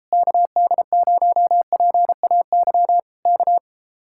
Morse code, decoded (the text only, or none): KD0PAY K